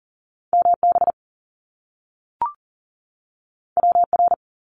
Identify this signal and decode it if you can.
Morse code, heard as a keyed tone, about 38 wpm, 700 Hz. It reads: MB WR
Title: Top Two Letter Combinations in Words 40wpm